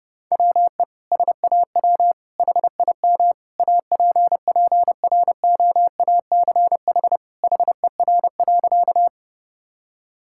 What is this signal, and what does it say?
Morse code, heard as a keyed tone, about 30 wpm, 700 Hz. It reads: WE SAW HIM APPROACH HER.
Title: Sentences from Top 1000 Words - Review Part 6 - 30wpm